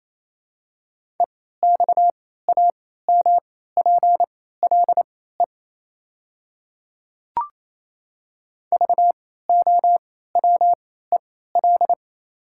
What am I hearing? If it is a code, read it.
Morse code, heard as a keyed tone, about 28 wpm, spaced as for 16 wpm, 700 Hz. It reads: EXAMPLE VOWEL
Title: 3x Character Spacing - Top 500 Words - 28wpm